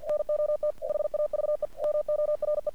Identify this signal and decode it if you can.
Morse code, heard as a keyed tone, about 48 wpm, 600 Hz. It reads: NOT HAVE MORE
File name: archived / CRSnet